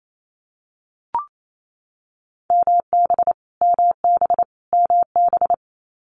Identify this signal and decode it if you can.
Morse code, heard as a keyed tone, about 28 wpm, 700 Hz. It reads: MB MB MB
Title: Multi-letter Phonemes Repeated 3x 28wpm